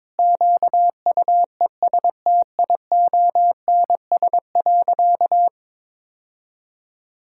Morse code, heard as a keyed tone, about 22 wpm, 700 Hz. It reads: QUESTIONS.
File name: practice / 22wpm / Sentences with 4-Word Phrases - 22wpm